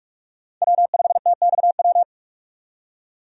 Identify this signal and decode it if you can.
Morse code, heard as a keyed tone, about 44 wpm, 700 Hz. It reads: W5TXW
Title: USA Call Signs - 5 Characters - 45wpm